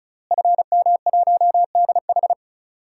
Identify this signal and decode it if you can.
Morse code, heard as a keyed tone, about 34 wpm, 700 Hz. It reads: FM1DH
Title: DXCC-100 Call Signs - 5 Characters - Repeated 2x - 35wpm